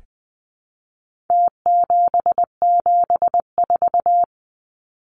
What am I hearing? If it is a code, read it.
Morse code, heard as a keyed tone, about 20 wpm, 700 Hz. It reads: T774
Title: Numbers - Contest Style - 4-Digit 20wpm